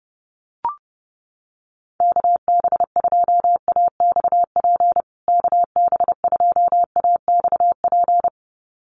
Morse code, heard as a keyed tone, about 30 wpm, 700 Hz. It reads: KB2AXP KB2AXP
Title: USA Call Signs Repeated 2x 30wpm